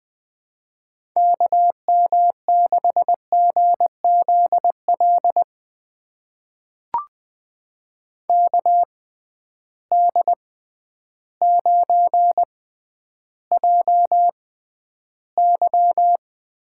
Morse code, heard as a keyed tone, about 20 wpm, spaced as for 11 wpm, 700 Hz. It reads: KM6GZL KD9JY
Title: USA Call Signs - 6 Characters - 6x Character Spacing - 20wpm